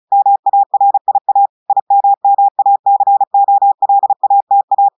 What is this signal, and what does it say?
Morse code, heard as a keyed tone, about 36 wpm, 800 Hz. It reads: MARIA IMMACOLATA
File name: audios / portugais / portugais texte en clair 1 vitesse 35 mots minutes